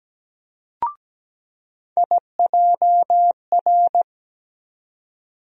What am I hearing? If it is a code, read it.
Morse code, heard as a keyed tone, about 17 wpm, 700 Hz. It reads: IJR
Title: USA Call Sign Postfixes - 3 Letters 17wpm